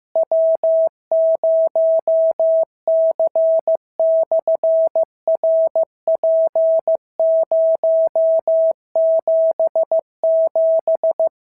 Morse code, heard as a keyed tone, about 15 wpm, 650 Hz. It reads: W0C/RP077
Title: SOTA QSO Practice 15wpm